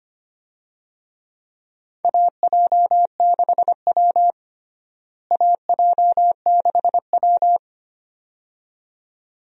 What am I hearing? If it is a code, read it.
Morse code, heard as a keyed tone, about 25 wpm, 700 Hz. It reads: AJ6W AJ6W
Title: USA Call Signs - 4 Characters - Repeated 2x - 25wpm